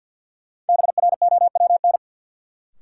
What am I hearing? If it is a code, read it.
Morse code, heard as a keyed tone, about 53 wpm, 700 Hz. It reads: BROWN
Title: Sentences from Top 1200 Words - New Word Drill - 50wpm